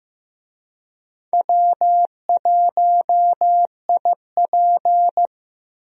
Morse code, heard as a keyed tone, about 15 wpm, 700 Hz. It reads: W1IP
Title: USA Call Signs - 4 Characters - 15wpm